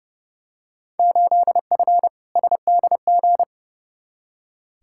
Morse code, heard as a keyed tone, about 30 wpm, 700 Hz. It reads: 8F SDG
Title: Field Day Exchanges Encore - 30wpm